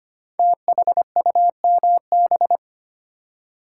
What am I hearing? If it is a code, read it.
Morse code, heard as a keyed tone, about 25 wpm, 700 Hz. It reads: THUMB
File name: 5 Letter Words - First 3 Letters Given - 25wpm